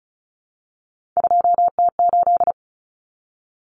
Morse code, heard as a keyed tone, about 34 wpm, 700 Hz. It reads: 2T8
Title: Common Cut Numbers - 3-Digit 35wpm